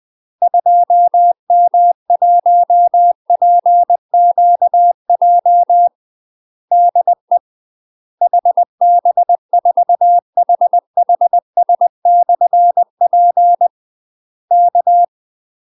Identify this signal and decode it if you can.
Morse code, heard as a keyed tone, about 20 wpm, 700 Hz. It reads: 2M1PQJ DE HB4HHS/P K
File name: UK MASTER QSOs TRAINING / 20WPM / UK MASTER QSOs 0056